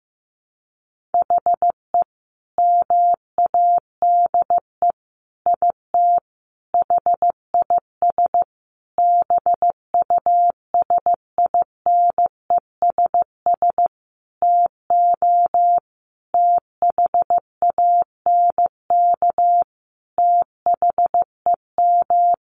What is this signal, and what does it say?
Morse code, heard as a keyed tone, about 14 wpm, 700 Hz. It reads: HE MADE IT HIS BUSINESS TO THANK THEM
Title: Sentences from Top 1400 Words 15wpm